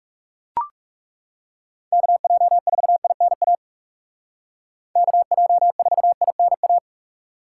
Morse code, heard as a keyed tone, about 37 wpm, 700 Hz. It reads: KJ4INA KJ4INA
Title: Speed Racing - USA 2x3 Call Signs 30wpm